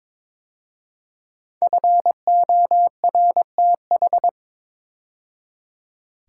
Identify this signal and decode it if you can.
Morse code, heard as a keyed tone, about 22 wpm, 700 Hz. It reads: FORTH
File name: 5 Letter Words - First 3 Letters Given - 22wpm